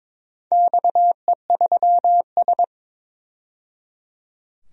Morse code, heard as a keyed tone, about 22 wpm, 700 Hz. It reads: XE3S
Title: DXCC-100 Call Signs - 4 Characters - 22wpm